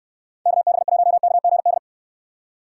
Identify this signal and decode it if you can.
Morse code, heard as a keyed tone, about 69 wpm, 700 Hz. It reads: KB1KGD